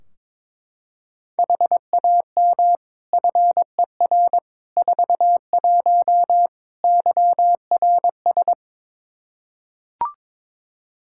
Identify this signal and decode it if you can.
Morse code, heard as a keyed tone, about 21 wpm, 700 Hz. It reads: HAM FER 41 YRS
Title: Practice QSO - Exchange 3 - More Info - 22wpm